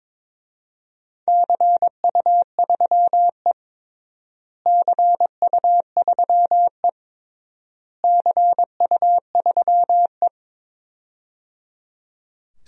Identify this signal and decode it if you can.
Morse code, heard as a keyed tone, about 22 wpm, 700 Hz. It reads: CU3E CU3E CU3E
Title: DXCC-100 Call Signs - 4 Characters - Repeated 3x - 22wpm